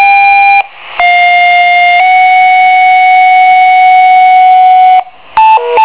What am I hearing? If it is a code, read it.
Two-tone page, heard as a one-second tone, then a three-second tone, about 710 then 740 Hz. It is heard from 1.0 to 5.1 s.